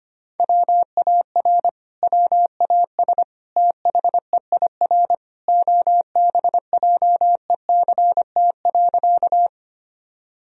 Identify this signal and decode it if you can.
Morse code, heard as a keyed tone, about 25 wpm, 700 Hz. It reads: WAR WAS THEIR OBJECT.
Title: Sentences from Top 1000 Words - Review Part 3 - 25wpm